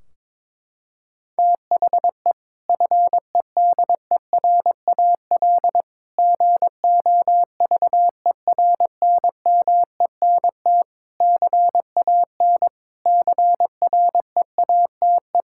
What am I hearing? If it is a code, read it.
Morse code, heard as a keyed tone, about 22 wpm, 700 Hz. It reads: THE FEDERAL GOVERNMENT CAN CREATE
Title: Sentences from Top 1600 Words 22wpm